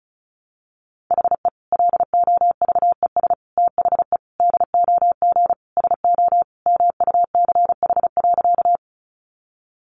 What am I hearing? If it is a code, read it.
Morse code, heard as a keyed tone, about 34 wpm, 700 Hz. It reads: HE LOVES THE DOG SO MUCH.
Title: Additional Easy Phrases 35wpm